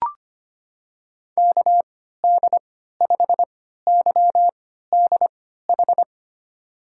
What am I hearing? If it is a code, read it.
Morse code, heard as a keyed tone, about 25 wpm, spaced as for 14 wpm, 700 Hz. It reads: KD5YDH
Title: USA Call Signs - 6 Characters - 3x Character Spacing - 25wpm